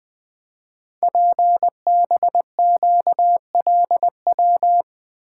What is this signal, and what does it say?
Morse code, heard as a keyed tone, about 20 wpm, 700 Hz. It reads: PBQLW
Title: Code Groups - 5 Letters - 20wpm